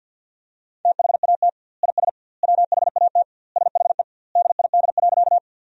Morse code, heard as a keyed tone, about 50 wpm, 700 Hz. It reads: THAT IS WHAT SHE DID.
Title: 2x Word-Spacing - Sentences from Top 100 Words - 50wpm